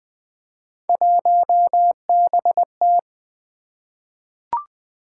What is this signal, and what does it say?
Morse code, heard as a keyed tone, about 20 wpm, 700 Hz. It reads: 1BT
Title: Three Letter-Number Combinations - 20wpm